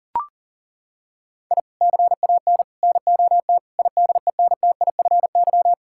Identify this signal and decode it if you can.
Morse code, heard as a keyed tone, about 37 wpm, 700 Hz. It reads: I CAN NOT IDENTIFY
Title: Sentences from Top 1000 Words - Review Part 6 - 40wpm